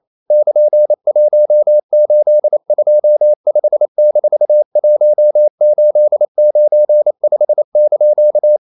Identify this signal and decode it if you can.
Morse code, heard as a keyed tone, about 28 wpm, 600 Hz. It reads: <KN>1825-1895)
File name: CW / Practice / EnglishQuotations0128